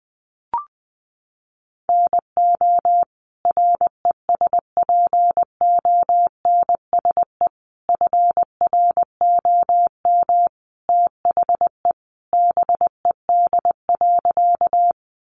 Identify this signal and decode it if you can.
Morse code, heard as a keyed tone, about 20 wpm, 700 Hz. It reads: NO RESPONSE FROM THE BED.
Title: Sentences from Top 1400 Words 20wpm